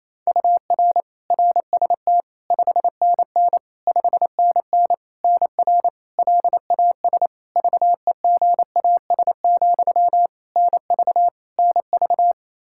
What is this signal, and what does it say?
Morse code, heard as a keyed tone, about 28 wpm, 700 Hz. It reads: UR RST 5NN 5NN NR LAS VEGAS, NV NV
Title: K1USN SST QSO Practice 28wpm